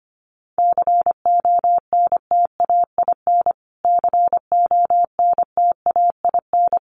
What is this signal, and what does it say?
Morse code, heard as a keyed tone, about 25 wpm, 700 Hz. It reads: CONTAIN CONTAIN
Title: Top 500 Words Repeated 25wpm